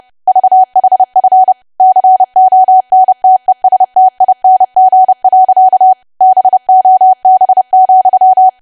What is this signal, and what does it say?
Morse code, heard as a keyed tone, about 30 wpm, 750 Hz. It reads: VHF CONTESTING. BOB,
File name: cwa / W1AW-Level3 / 30WPM